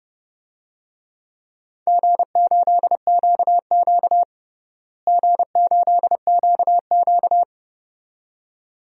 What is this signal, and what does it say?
Morse code, heard as a keyed tone, about 30 wpm, 700 Hz. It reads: G8QQ G8QQ
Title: DXCC-100 Call Signs - 4 Characters - Repeated 2x - 30wpm